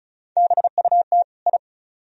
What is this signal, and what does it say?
Morse code, heard as a keyed tone, about 46 wpm, 700 Hz. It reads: BUT I